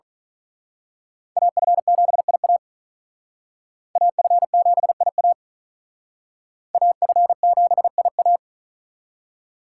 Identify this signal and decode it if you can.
Morse code, heard as a keyed tone, about 41 wpm, 700 Hz. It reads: AF7IA AF7IA AF7IA
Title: Speed Racing - USA Call Signs 35wpm